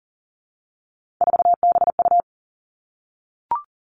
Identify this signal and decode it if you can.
Morse code, heard as a keyed tone, about 43 wpm, 700 Hz. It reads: 4BU
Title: Three Challenging Letter-Number Combinations 40wpm